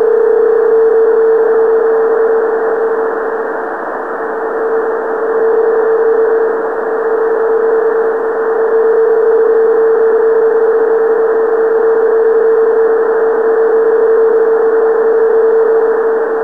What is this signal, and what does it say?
PSK31, about 31.25 baud.